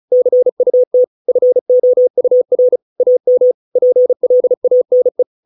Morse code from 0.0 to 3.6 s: CUT FOUR AM